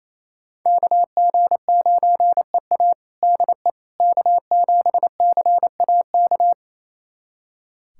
Morse code, heard as a keyed tone, about 28 wpm, 700 Hz. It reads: KG9EA DE K7CAK
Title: USA Callsign DE Callsign - 5 Characters - 28wpm